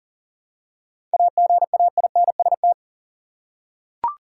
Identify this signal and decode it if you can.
Morse code, heard as a keyed tone, about 40 wpm, 700 Hz. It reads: AGAINST
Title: Seven Letter Words - 40wpm